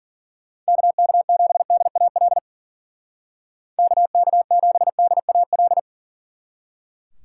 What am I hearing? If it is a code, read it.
Morse code, heard as a keyed tone, about 49 wpm, 700 Hz. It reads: KK7DAL KK7DAL
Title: Speed Racing - USA 2x3 Call Signs 40wpm